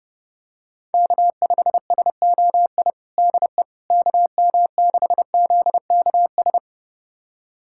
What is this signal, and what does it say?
Morse code, heard as a keyed tone, about 30 wpm, 700 Hz. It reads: K5SOI DE KM6ZKS
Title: USA Callsign DE Callsign 30wpm